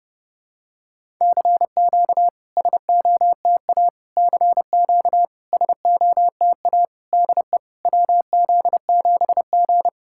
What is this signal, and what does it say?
Morse code, heard as a keyed tone, about 29 wpm, 690 Hz. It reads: CQ SOTA CQ SOTA DE WZ7G
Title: SOTA QSO Practice 30wpm